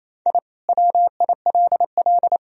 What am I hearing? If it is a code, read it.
Morse code, heard as a keyed tone, about 28 wpm, 700 Hz. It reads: I WILL
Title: Sentences from Top 1000 Words - Review Part 3 - 28wpm